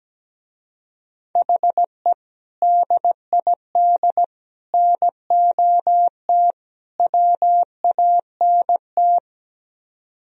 Morse code, heard as a keyed tone, about 17 wpm, 700 Hz. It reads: HE DID NOT WANT
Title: Speed-Racing - Sets of 4 Words 17wpm